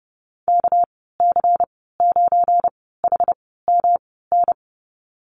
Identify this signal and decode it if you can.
Morse code, heard as a keyed tone, about 30 wpm, spaced as for 17 wpm, 700 Hz. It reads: KC9HMN
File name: Lesson 37g - Letter x - Call Signs 30wpm